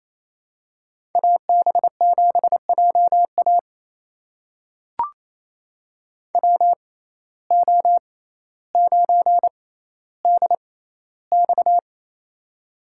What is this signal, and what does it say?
Morse code, heard as a keyed tone, about 28 wpm, spaced as for 14 wpm, 700 Hz. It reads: AB7JA WO9DX
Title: USA Call Signs - 5 Characters - 6x Character Spacing - 28wpm